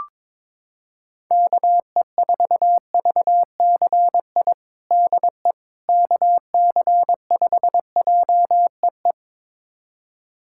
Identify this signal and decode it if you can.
Morse code, heard as a keyed tone, about 22 wpm, 700 Hz. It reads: KE4VCI DE KC5JEE
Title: USA Callsign DE Callsign - 6 Characters - 22wpm